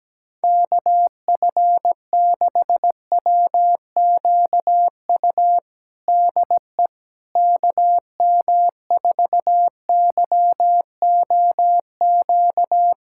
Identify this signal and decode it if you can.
Morse code, heard as a keyed tone, about 17 wpm, 700 Hz. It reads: KF6WQU DE KM4YOQ
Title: USA Callsign DE Callsign - 6 Characters - 17wpm